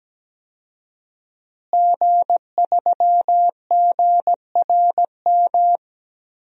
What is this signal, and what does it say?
Morse code, heard as a keyed tone, about 17 wpm, 700 Hz. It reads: G3GRM
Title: DXCC-100 Call Signs - 5 Characters - Repeated 3x - 17wpm